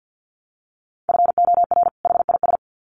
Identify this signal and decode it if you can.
Morse code, heard as a keyed tone, about 50 wpm, 700 Hz. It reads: FOR HIS
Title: Sets of 2 Words 50wpm